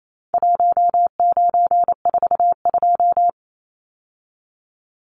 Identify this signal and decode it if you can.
Morse code, heard as a keyed tone, about 28 wpm, 700 Hz. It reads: 1942
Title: Numbers - Contest Style - 4-Digit 28wpm